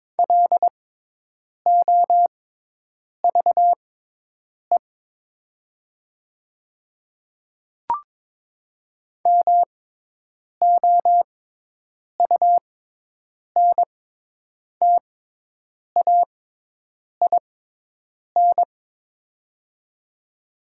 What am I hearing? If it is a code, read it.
Morse code, heard as a keyed tone, about 22 wpm, spaced as for 8 wpm, 700 Hz. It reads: LOVE MOUNTAIN
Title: 6x Character Spacing - Top 500 Words - 22wpm